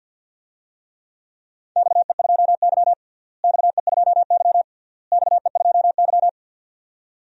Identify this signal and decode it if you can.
Morse code, heard as a keyed tone, about 51 wpm, 700 Hz. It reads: XE2Y XE2Y XE2Y
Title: DXCC-100 Call Signs - 4 Characters - Repeated 3x - 50wpm